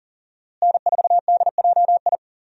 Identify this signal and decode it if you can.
Morse code, heard as a keyed tone, about 40 wpm, 700 Hz. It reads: N4DJI
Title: USA Call Signs - 5 Characters - Repeated 3x - 40wpm